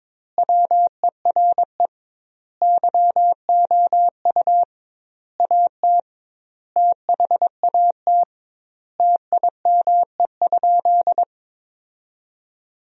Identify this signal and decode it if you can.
Morse code, heard as a keyed tone, about 22 wpm, 700 Hz. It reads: WERE YOU AT THAT TIME?
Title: 2x Word-Spacing - Sentences from Top 200 Words - 22wpm